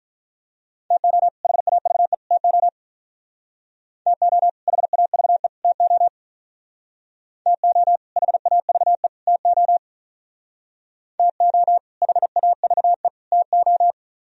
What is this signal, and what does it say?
Morse code, heard as a keyed tone, about 44 wpm, 700 Hz. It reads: TO HAVE TO TO HAVE TO TO HAVE TO TO HAVE TO
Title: Speed-Racing - Sets of 3 Words 35wpm